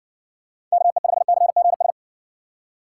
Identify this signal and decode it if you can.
Morse code, heard as a keyed tone, about 60 wpm, 700 Hz. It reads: KE5CGS